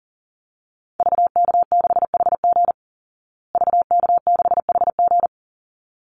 Morse code, heard as a keyed tone, about 40 wpm, 700 Hz. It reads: VK6HG VK6HG
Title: Call Signs Repeated 3x - 40wpm